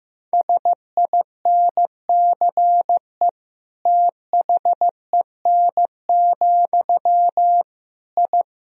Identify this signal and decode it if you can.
Morse code, heard as a keyed tone, about 16 wpm, 700 Hz. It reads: SINCE THEN, I
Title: Sentences from Top 1000 Words - Review Part 1 - 15wpm